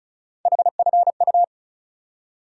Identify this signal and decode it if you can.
Morse code, heard as a keyed tone, about 36 wpm, 700 Hz. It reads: HFU
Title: USA Call Sign Postfixes - 3 Letters 35wpm